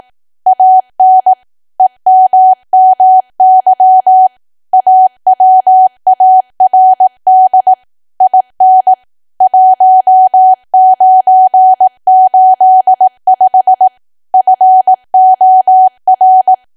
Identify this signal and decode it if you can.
Morse code, heard as a keyed tone, about 18 wpm, 750 Hz. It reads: AN EMMY AWARD IN 1985 FOR
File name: Archive / 18 WPM / 18WPM